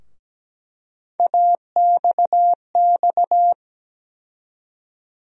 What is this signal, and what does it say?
Morse code, heard as a keyed tone, about 17 wpm, 700 Hz. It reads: AXX
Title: USA Call Sign Postfixes - 3 Letters 17wpm